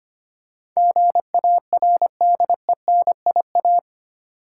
Morse code, heard as a keyed tone, about 25 wpm, 700 Hz. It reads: GARDENIA
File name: practice / 25wpm / Difficult Words with Hints - 25wpm